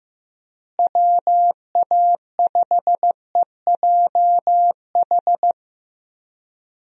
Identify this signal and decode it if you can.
Morse code, heard as a keyed tone, about 15 wpm, 700 Hz. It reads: WA5EJH
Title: USA Call Signs - 6 Characters - Repeated 2x - 15wpm